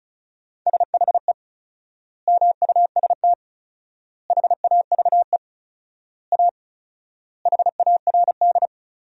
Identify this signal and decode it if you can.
Morse code, heard as a keyed tone, about 34 wpm, 700 Hz. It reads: SHE MUST HAVE A HARD
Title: 4x Word-Spacing - Sentences from Top 300 Words - 35wpm